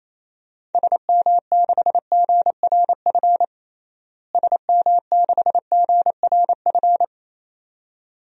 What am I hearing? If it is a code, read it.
Morse code, heard as a keyed tone, about 28 wpm, 700 Hz. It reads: SM6GRF SM6GRF
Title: DXCC-100 Call Signs - 6 Characters - Repeated 2x - 28wpm